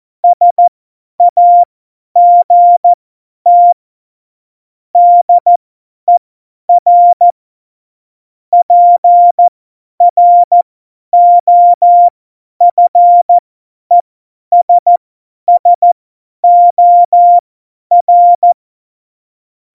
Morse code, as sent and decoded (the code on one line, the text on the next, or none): ... .- --. - / -.. .-. / .--. .-. --- ..-. ... ... --- .-.
SAGT DER PROFESSOR